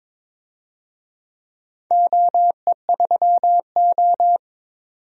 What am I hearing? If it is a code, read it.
Morse code, heard as a keyed tone, about 22 wpm, 700 Hz. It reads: OE3O
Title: DXCC-100 Call Signs - 4 Characters - 22wpm